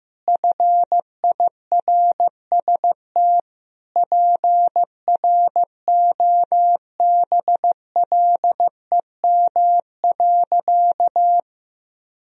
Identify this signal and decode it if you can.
Morse code, heard as a keyed tone, about 15 wpm, 700 Hz. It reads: FIRST PROBLEM.
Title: Sentences from Top 1000 Words - Review Part 2 - 15wpm